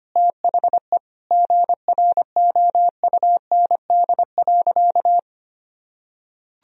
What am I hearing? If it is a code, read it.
Morse code, heard as a keyed tone, about 25 wpm, 700 Hz. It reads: THE GROUND.
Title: Sentences from Top 1000 Words - Review Part 2 - 25wpm